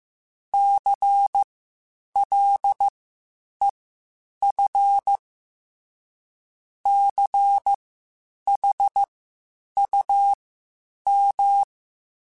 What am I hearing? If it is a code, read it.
Morse code, heard as a keyed tone, about 15 wpm, spaced as for 8 wpm, 800 Hz. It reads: CLEF CHUM
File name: Spaced Alphas / Spaced Alphas mp3s / Cs sp @13